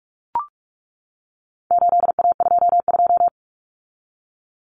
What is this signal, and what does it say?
Morse code, heard as a keyed tone, about 44 wpm, 700 Hz. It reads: 8A22
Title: Common Cut Numbers - 4-Digit 45wpm